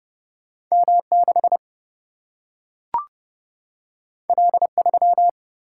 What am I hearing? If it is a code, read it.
Morse code, heard as a keyed tone, about 30 wpm, 700 Hz. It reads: M 6 L 3
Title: Single-Letter Single-Number 30wpm